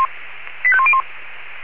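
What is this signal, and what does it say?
Five-tone selective call (ZVEI), heard as quick runs of five tones, tones of about 70 ms. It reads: 85101
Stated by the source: ZVEI five-tone selective call received on SDR.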